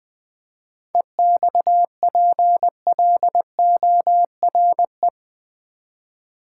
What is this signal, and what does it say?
Morse code, heard as a keyed tone, about 20 wpm, 700 Hz. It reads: EXPLORE
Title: Sentences from Top 1100 Words - New Word Drill v2 - 20wpm